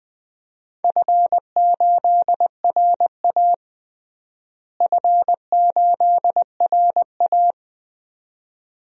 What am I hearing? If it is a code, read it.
Morse code, heard as a keyed tone, about 20 wpm, 700 Hz. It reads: F8RA F8RA
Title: DXCC-100 Call Signs - 4 Characters - Repeated 2x - 20wpm